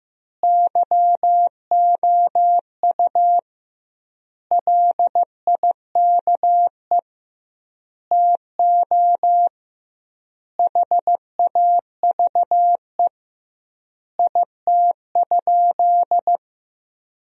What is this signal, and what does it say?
Morse code, heard as a keyed tone, about 15 wpm, 700 Hz. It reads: YOU LIKE TO HAVE IT?
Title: 2x Word-Spacing - Sentences from Top 100 Words - 15wpm